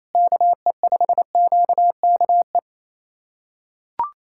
Morse code, heard as a keyed tone, about 28 wpm, 700 Hz. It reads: KE5QKE
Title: USA Callsign DE Callsign - 6 Characters - 28wpm